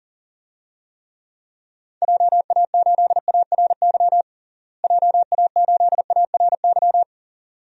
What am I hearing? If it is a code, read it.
Morse code, heard as a keyed tone, about 40 wpm, 700 Hz. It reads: JA8ARY JA8ARY
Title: DXCC-100 Call Signs - 6 Characters - Repeated 2x - 40wpm